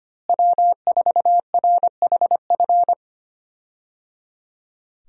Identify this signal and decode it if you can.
Morse code, heard as a keyed tone, about 25 wpm, 700 Hz. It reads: W4RHF
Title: USA Call Signs - 5 Characters - Repeated 3x - 25wpm